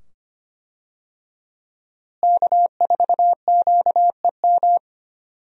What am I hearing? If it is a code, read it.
Morse code, heard as a keyed tone, about 25 wpm, 700 Hz. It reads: K4QEM
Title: USA Call Signs - 5 Characters - Repeated 2x - 25wpm